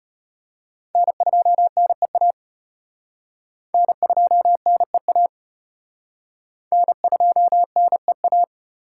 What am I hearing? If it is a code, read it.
Morse code, heard as a keyed tone, about 33 wpm, 700 Hz. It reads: N2NEA N2NEA N2NEA